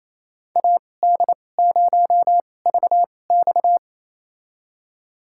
Morse code, heard as a keyed tone, about 27 wpm, spaced as for 20 wpm, 700 Hz. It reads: AD0VX
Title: USA Call Signs - 2x Character Spacing - 28wpm